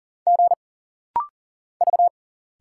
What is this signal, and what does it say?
Morse code, heard as a keyed tone, about 40 wpm, 700 Hz. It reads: G V
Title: Single Letters - Rapid-Fire 40wpm